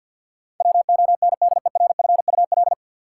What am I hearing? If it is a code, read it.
Morse code, heard as a keyed tone, about 53 wpm, 700 Hz. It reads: WONDERFUL
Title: Sentences from Top 1100 Words - New Word Drill Alt - 50wpm